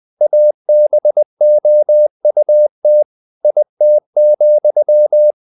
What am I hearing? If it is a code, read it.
Morse code, heard as a keyed tone, about 20 wpm, 600 Hz. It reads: ABOUT IT,
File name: the old man and the sea part 2 20wpm